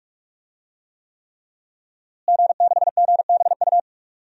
Morse code, heard as a keyed tone, about 46 wpm, 700 Hz. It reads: G6GBU
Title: DXCC-100 Call Signs - 5 Characters - Repeated 3x - 45wpm